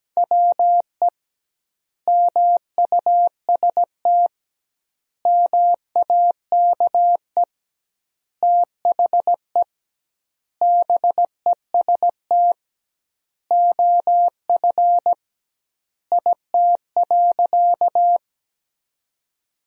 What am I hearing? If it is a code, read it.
Morse code, heard as a keyed tone, about 17 wpm, 700 Hz. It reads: WE MUST MAKE THE BEST OF IT.
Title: 2x Word-Spacing - Sentences from Top 400 Words - 17wpm